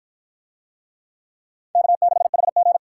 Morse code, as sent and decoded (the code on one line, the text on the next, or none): -.- -.... .... --.
K6HG